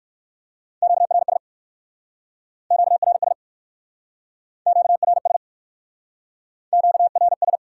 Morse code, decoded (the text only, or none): QRS QRS QRS QRS